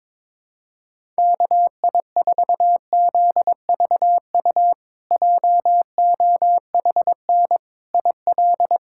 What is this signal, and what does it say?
Morse code, heard as a keyed tone, about 22 wpm, 700 Hz. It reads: KI4ZVU JOHN IL
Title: K1USN SST QSO Practice - 22wpm